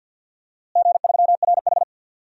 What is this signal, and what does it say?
Morse code, heard as a keyed tone, about 47 wpm, 700 Hz. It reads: G3RH
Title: DXCC-100 Call Signs - 4 Characters - Repeated 3x - 50wpm